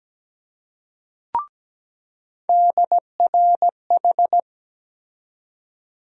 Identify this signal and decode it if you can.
Morse code, heard as a keyed tone, about 17 wpm, 700 Hz. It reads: DRH
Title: USA Call Sign Postfixes - 3 Letters 17wpm